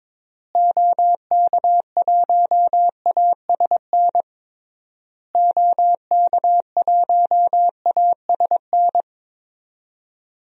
Morse code, heard as a keyed tone, about 22 wpm, 700 Hz. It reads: OK1ASN OK1ASN
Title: DXCC-100 Call Signs - 6 Characters - Repeated 3x - 22wpm